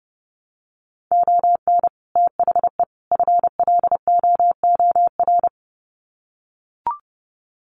Morse code, heard as a keyed tone, about 30 wpm, 700 Hz. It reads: ON THE FLOOR
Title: Sets of 3 Words 30wpm